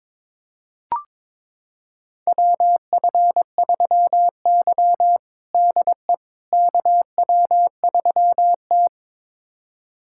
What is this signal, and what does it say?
Morse code, heard as a keyed tone, about 22 wpm, 700 Hz. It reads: WF3Y DE KW3T
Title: USA Callsign DE Callsign - 4 Characters - 22wpm